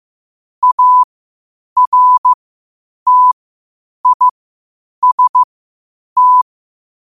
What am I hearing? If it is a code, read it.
Morse code, heard as a keyed tone, about 15 wpm, spaced as for 8 wpm, 1000 Hz. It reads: ARTIST
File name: qotdfiles / 8wpm